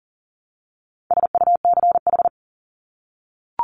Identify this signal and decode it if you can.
Morse code, heard as a keyed tone, about 40 wpm, 700 Hz. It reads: SUCH
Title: Top 200 Words Repeated 40wpm